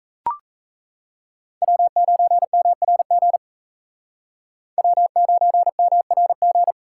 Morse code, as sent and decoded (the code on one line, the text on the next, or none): .-- ----. -- .-. --. / .-- ----. -- .-. --.
W9MRG W9MRG